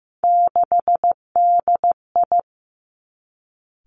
Morse code, heard as a keyed tone, about 15 wpm, 700 Hz. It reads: 6DI
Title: Three Challenging Letter-Number Combinations 15wpm